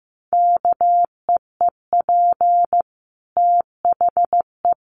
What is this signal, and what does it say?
Morse code, heard as a keyed tone, about 15 wpm, 700 Hz. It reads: KEEP THE
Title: Sentences from Top 1400 Words 15wpm